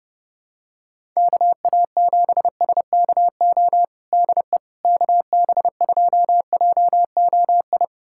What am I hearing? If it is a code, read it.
Morse code, heard as a keyed tone, about 32 wpm, 700 Hz. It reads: KA7SKO DE KB2JOI
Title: USA Callsign DE Callsign - 6 Characters - 30wpm